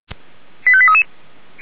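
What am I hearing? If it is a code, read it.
Five-tone selective call (ZVEI), heard as quick runs of five tones, tones of about 70 ms. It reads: 85833
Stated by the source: ZVEI five-tone selective call received on SDR.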